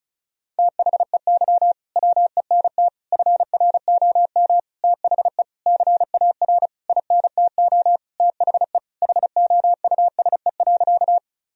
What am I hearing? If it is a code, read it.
Morse code, heard as a keyed tone, about 36 wpm, 700 Hz. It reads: THEY WENT FROM THE CAR INTO THE HOUSE.
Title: Sentences from Top 1000 Words - Review Part 1 - 35wpm